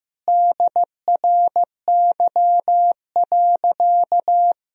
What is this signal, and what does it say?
Morse code, heard as a keyed tone, about 15 wpm, 700 Hz. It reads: DRY.
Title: Sentences from Top 1000 Words - Review Part 3 - 15wpm